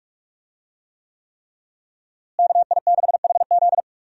Morse code, heard as a keyed tone, about 43 wpm, 700 Hz. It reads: KI6HZ
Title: USA Call Signs - 5 Characters - Repeated 3x - 45wpm